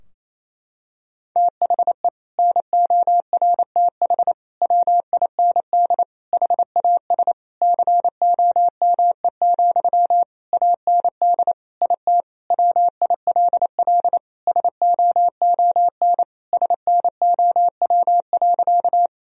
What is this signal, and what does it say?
Morse code, heard as a keyed tone, about 28 wpm, 700 Hz. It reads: THE NORTH WIND HAS COME, AND IT WILL SOON SNOW.
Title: Sentences from Top 1000 Words - Review Part 3 - 28wpm